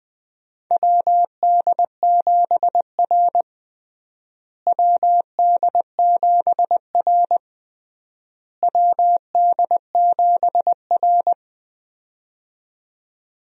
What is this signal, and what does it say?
Morse code, heard as a keyed tone, about 20 wpm, 700 Hz. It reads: WD7R WD7R WD7R
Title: USA Call Signs - 4 Characters - Repeated 3x - 20wpm